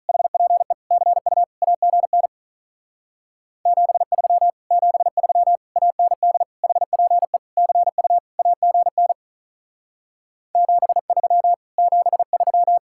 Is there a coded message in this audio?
Yes: Morse code from 0.1 to 12.8 s, heard as a keyed tone, about 44 wpm, 700 Hz. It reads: HPE CU AGN 73 73 AND HPE CU AGN 73 73